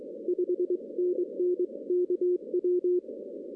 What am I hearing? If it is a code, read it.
Morse code, heard as a keyed tone, about 23 wpm, 360 Hz. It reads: 5NN KW
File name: 2016 arrl dx cw / 2016-ARRL-DX-CW-K5TR- - 0-25